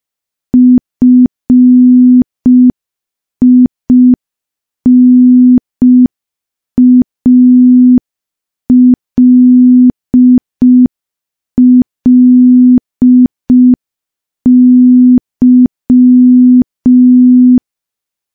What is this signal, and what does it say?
Morse code, heard as a keyed tone, about 5 wpm, 250 Hz. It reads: FINALLY
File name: morse / morse-training-05WPM